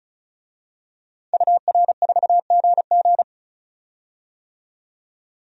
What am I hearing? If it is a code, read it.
Morse code, heard as a keyed tone, about 34 wpm, 700 Hz. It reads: UR4GG
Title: DXCC-100 Call Signs - 5 Characters - 35wpm